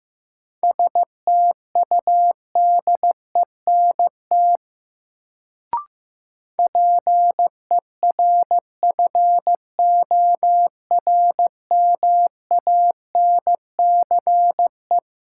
Morse code, heard as a keyed tone, about 15 wpm, 700 Hz. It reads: STUDENT PERFORMANCE
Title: Sentences from Top 1000 Words - New Word Drill - 15wpm